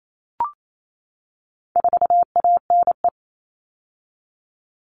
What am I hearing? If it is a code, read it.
Morse code, heard as a keyed tone, about 28 wpm, 700 Hz. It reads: 4ANE
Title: All Cut Numbers - 4-Digit 28wpm